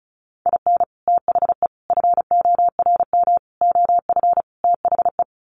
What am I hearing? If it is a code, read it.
Morse code, heard as a keyed tone, about 34 wpm, 700 Hz. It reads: IN THE FORM OF THE
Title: Sets of 5 Words Encore2 35wpm